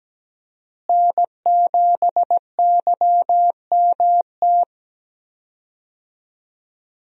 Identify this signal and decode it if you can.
Morse code, heard as a keyed tone, about 17 wpm, 700 Hz. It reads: N7YMT
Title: USA Call Signs - 5 Characters - Repeated 2x - 17wpm